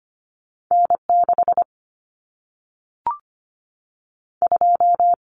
Morse code, heard as a keyed tone, about 25 wpm, 700 Hz. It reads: N 6 2
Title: Common Cut Numbers - 2-Digit 25wpm